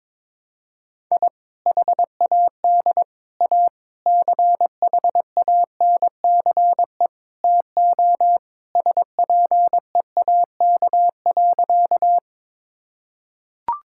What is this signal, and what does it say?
Morse code, heard as a keyed tone, about 22 wpm, 700 Hz. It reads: I HAD A CHANCE TO SPEAK.
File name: practice / 22wpm / Sentences with 4-Word Phrases - 22wpm